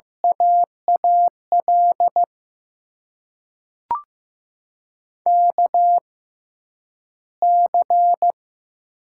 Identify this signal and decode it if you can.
Morse code, heard as a keyed tone, about 15 wpm, spaced as for 9 wpm, 700 Hz. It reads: AAL KC
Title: USA Call Signs - 6 Characters - 6x Character Spacing - 15wpm